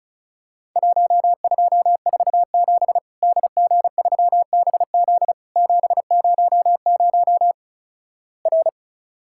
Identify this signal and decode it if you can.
Morse code, heard as a keyed tone, about 36 wpm, 700 Hz. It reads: 1247 DG3BZ 700 R
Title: Worked All Europe DX Contest - QTC Exchange - 35wpm